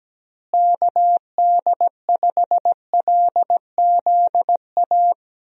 Morse code, heard as a keyed tone, about 17 wpm, 700 Hz. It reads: KD5LZA
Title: USA Call Signs - 6 Characters - Repeated 3x - 17wpm